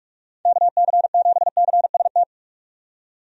Morse code, heard as a keyed tone, about 42 wpm, 700 Hz. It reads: KC7CST
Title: USA Call Signs - 6 Characters - 45wpm